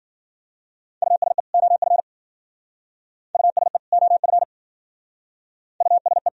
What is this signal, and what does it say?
Morse code, heard as a keyed tone, about 50 wpm, 700 Hz. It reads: USE OF USE OF USE
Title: Speed-Racing - Sets of 2 Words 40wpm